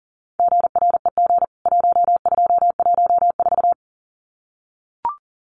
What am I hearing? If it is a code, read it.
Morse code, heard as a keyed tone, about 40 wpm, 700 Hz. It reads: GREG 1214
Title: CWT Exchange 40wpm